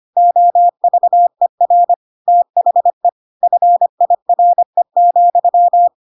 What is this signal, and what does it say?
Morse code, heard as a keyed tone, about 25 wpm, 700 Hz. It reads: OVER THE FIRE,
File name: Pinocchio / Pinocchio ch3 25 25 ik1hge